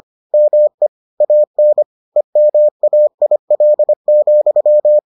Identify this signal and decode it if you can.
Morse code, heard as a keyed tone, about 25 wpm, 600 Hz. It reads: ME AN EMAIL,